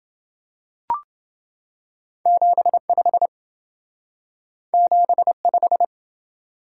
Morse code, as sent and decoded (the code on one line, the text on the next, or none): --... ..... / --... .....
75 75